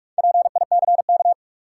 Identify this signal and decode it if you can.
Morse code, heard as a keyed tone, about 46 wpm, 700 Hz. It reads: PICK
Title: Sentences from Top 1000 Words - New Word Drill - 45wpm